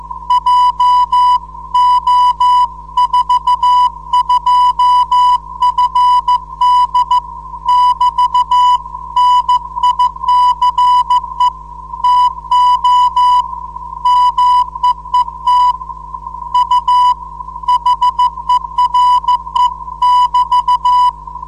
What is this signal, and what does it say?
Morse code, heard as a keyed tone, about 14 wpm, 990 Hz. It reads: JO42FD <BT> NICE TO MEET U HERE <BT>
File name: 472khz qso web / 29march2014 / 01 dl4yhf 29mar2014 2248utc